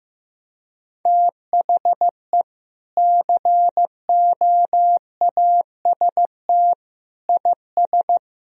Morse code, heard as a keyed tone, about 15 wpm, 700 Hz. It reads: THE COAST IS
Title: Sentences from Top 1000 Words - Review Part 4 - 15wpm